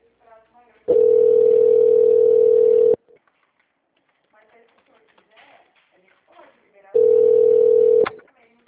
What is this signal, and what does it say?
Ringback tone (North America), 440 + 480 Hz, with speech.
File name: out- -2021- - - .25